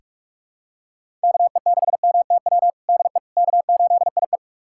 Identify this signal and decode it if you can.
Morse code, heard as a keyed tone, about 41 wpm, 700 Hz. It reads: KE6MTW DE K8IE
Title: USA Callsign DE Callsign 45wpm